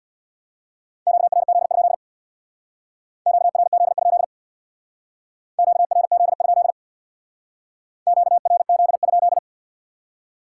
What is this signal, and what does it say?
Morse code, heard as a keyed tone, about 65 wpm, 700 Hz. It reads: QRZ? QRZ? QRZ? QRZ?